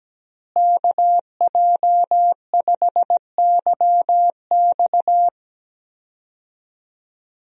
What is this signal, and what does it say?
Morse code, heard as a keyed tone, about 17 wpm, 700 Hz. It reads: KJ5YX
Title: USA Call Signs - 5 Characters - Repeated 2x - 17wpm